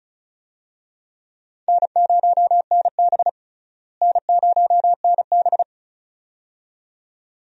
Morse code, heard as a keyed tone, about 36 wpm, 700 Hz. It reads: N0NB N0NB
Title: USA Call Signs - 4 Characters - Repeated 2x - 35wpm